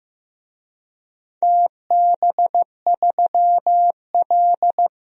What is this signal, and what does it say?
Morse code, heard as a keyed tone, about 16 wpm, 700 Hz. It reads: TB3L
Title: DXCC-100 Call Signs - 4 Characters - Repeated 3x - 15wpm